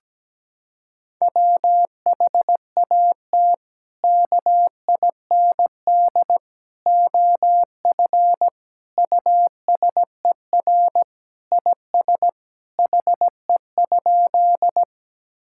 Morse code, heard as a keyed tone, about 17 wpm, 700 Hz. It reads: WHAT KIND OF USER IS HE?
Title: Sentences from Top 1000 Words - Review Part 5 - 17wpm